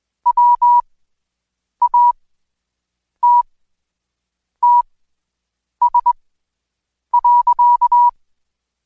Morse code, heard as a keyed tone, about 20 wpm, spaced as for 7 wpm, 970 Hz. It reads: WATTS.
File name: Code Quick / AudioDownld2 / Seg112-Day17-DayDrill 17